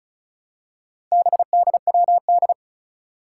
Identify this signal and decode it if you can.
Morse code, heard as a keyed tone, about 36 wpm, 700 Hz. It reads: BDWD
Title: Four Challenging Letter-Number Combinations 35wpm